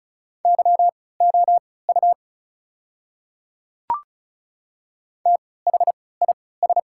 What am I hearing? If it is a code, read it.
Morse code, heard as a keyed tone, about 36 wpm, spaced as for 20 wpm, 700 Hz. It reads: YOU THIS
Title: 3x Character Spacing - Top 100 Words - 35wpm